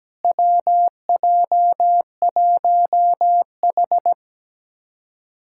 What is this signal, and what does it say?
Morse code, heard as a keyed tone, about 17 wpm, 700 Hz. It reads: WJ1H
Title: USA Callsign DE Callsign - 4 Characters - 17wpm